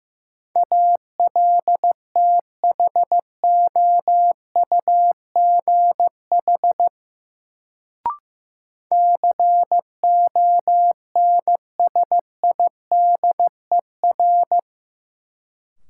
Morse code, heard as a keyed tone, about 16 wpm, 700 Hz. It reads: ALTHOUGH CONSIDER
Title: Sentences from Top 600 Words - New Word Drill - 15wpm